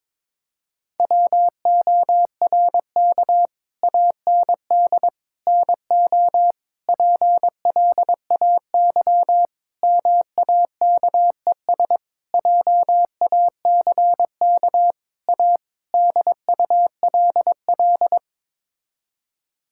Morse code, heard as a keyed tone, about 22 wpm, 700 Hz. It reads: WORK AND NO PLAY MAKES JACK A DULL
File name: practice / 22wpm / Sets of 8 Words - 22wpm